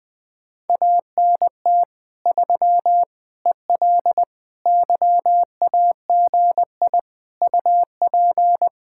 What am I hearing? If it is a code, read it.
Morse code, heard as a keyed tone, about 21 wpm, 700 Hz. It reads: ANT 3 EL YAGI UP
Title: Speed-Racing Common QSO Phrases - 20wpm